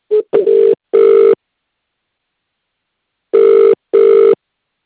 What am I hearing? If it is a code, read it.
Ringback tone (UK), 400 + 450 Hz.